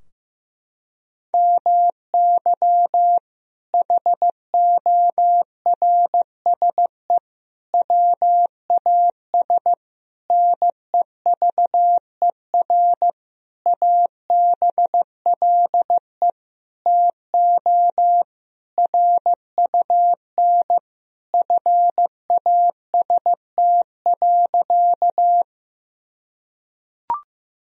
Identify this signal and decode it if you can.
Morse code, heard as a keyed tone, about 15 wpm, 700 Hz. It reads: MY HORSE WAS NEVER ABLE TO RUN FAST.
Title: Sentences from Top 1000 Words - Review Part 2 - 15wpm